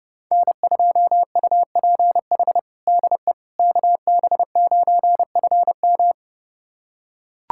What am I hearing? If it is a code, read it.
Morse code, heard as a keyed tone, about 32 wpm, 700 Hz. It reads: N2UPH DE KB9FM
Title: USA Callsign DE Callsign - 5 Characters - 30wpm